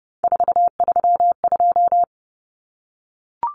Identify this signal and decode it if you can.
Morse code, heard as a keyed tone, about 30 wpm, 700 Hz. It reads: 432
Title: Common Cut Numbers - 3-Digit 30wpm